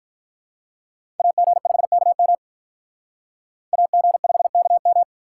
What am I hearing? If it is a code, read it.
Morse code, heard as a keyed tone, about 45 wpm, 700 Hz. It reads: AG5KM AG5KM